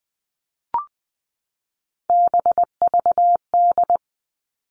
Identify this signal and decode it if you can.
Morse code, heard as a keyed tone, about 20 wpm, 700 Hz. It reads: BVD
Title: All Cut Numbers - 3-Digit 20wpm